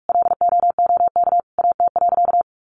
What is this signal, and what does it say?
Morse code, heard as a keyed tone, about 44 wpm, 700 Hz. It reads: LOOK AT.
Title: Sentences from Top 200 Words 45wpm